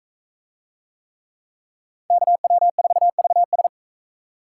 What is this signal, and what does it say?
Morse code, heard as a keyed tone, about 40 wpm, 700 Hz. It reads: KW4VS